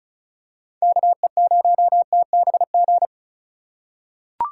Morse code, heard as a keyed tone, about 34 wpm, 700 Hz. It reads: KE0TBG
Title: USA Call Signs - 6 Characters - 6x Character Spacing - 35wpm